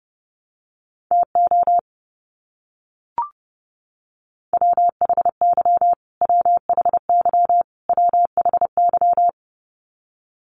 Morse code, heard as a keyed tone, about 30 wpm, 700 Hz. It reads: TO WHY WHY WHY
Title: Top 500 Words Repeated 30wpm